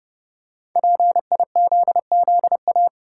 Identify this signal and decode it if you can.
Morse code, heard as a keyed tone, about 30 wpm, 700 Hz. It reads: PIZZA
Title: Five Letter Words - 30wpm